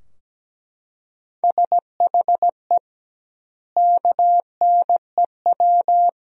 Morse code, heard as a keyed tone, about 17 wpm, 700 Hz. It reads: SHE KNEW
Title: 2x Word-Spacing - Sentences from Top 400 Words - 17wpm